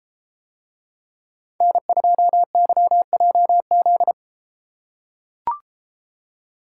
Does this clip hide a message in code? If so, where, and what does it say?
Yes: Morse code from 1.6 to 4.1 s, heard as a keyed tone, about 33 wpm, 700 Hz. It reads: N2YJZ